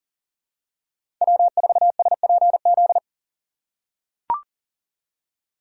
Morse code, heard as a keyed tone, about 40 wpm, 700 Hz. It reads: W4SPZ
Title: USA Call Signs - 5 Characters - 4x Character Spacing - 40wpm